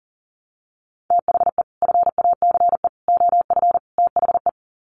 Morse code, heard as a keyed tone, about 40 wpm, 700 Hz. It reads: THE FACE OF THE
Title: Sets of 4 Words 40wpm